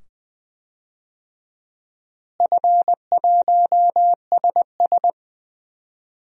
Morse code, heard as a keyed tone, about 20 wpm, 700 Hz. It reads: F1SS
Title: DXCC-100 Call Signs - 4 Characters - Repeated 2x - 20wpm